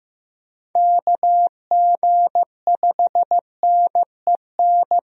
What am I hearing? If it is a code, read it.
Morse code, heard as a keyed tone, about 15 wpm, 700 Hz. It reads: KG5NEN
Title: USA Call Signs - 6 Characters - Repeated 2x - 15wpm